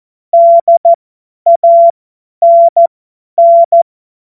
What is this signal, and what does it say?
Morse code, heard as a keyed tone, about 14 wpm, spaced as for 10 wpm, 680 Hz. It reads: DANN
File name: Morsekurs / Witze / Witz 27. Juli 2021, 50 F70